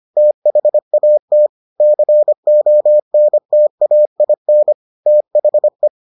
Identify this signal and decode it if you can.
Morse code, heard as a keyed tone, about 25 wpm, 600 Hz. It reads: THAT CONTAIN THE